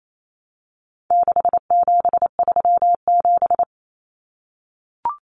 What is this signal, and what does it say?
Morse code, heard as a keyed tone, about 28 wpm, 700 Hz. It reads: 6737
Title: Common Cut Numbers - 4-Digit 28wpm